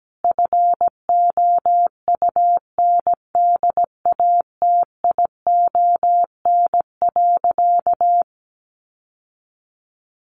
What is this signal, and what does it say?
Morse code, heard as a keyed tone, about 17 wpm, 700 Hz. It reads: FOUNDATION.
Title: Sentences from Top 1200 Words 17wpm